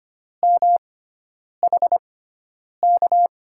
Morse code, heard as a keyed tone, about 25 wpm, spaced as for 9 wpm, 700 Hz. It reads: MHK
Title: USA Call Signs - 6 Characters - 6x Character Spacing - 25wpm